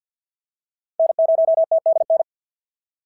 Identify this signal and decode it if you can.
Morse code, heard as a keyed tone, about 46 wpm, 690 Hz. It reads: N0TDN
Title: SOTA QSO Practice 50wpm